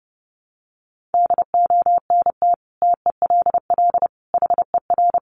Morse code, heard as a keyed tone, about 30 wpm, 700 Hz. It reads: DONT TELL HER
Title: Sentences from Top 700 Words 30wpm